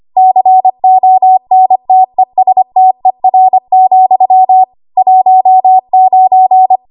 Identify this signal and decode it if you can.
Morse code, heard as a keyed tone, about 25 wpm, 750 Hz. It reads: CONTESTER, 19
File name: Archive / 25 WPM / 25WPM